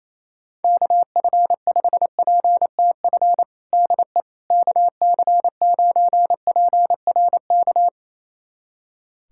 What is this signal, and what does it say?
Morse code, heard as a keyed tone, about 28 wpm, 700 Hz. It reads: KF5PTF DE KC9PRK
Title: USA Callsign DE Callsign - 6 Characters - 28wpm